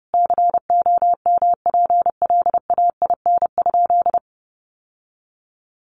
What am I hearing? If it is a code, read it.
Morse code, heard as a keyed tone, about 28 wpm, 700 Hz. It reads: COMPLAIN?
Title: Sentences from Top 1300 Words 30wpm